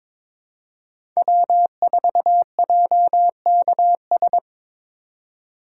Morse code, heard as a keyed tone, about 22 wpm, 700 Hz. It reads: W4JKS
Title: USA Call Signs - 3x Character Spacing - 22wpm